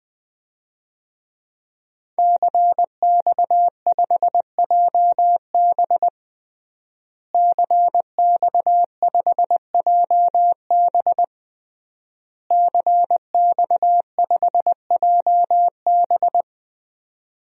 Morse code, heard as a keyed tone, about 21 wpm, 700 Hz. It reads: CX5JB CX5JB CX5JB
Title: DXCC-100 Call Signs - 5 Characters - Repeated 3x - 20wpm